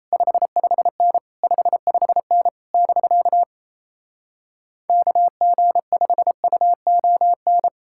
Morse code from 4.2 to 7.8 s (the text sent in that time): KG5UON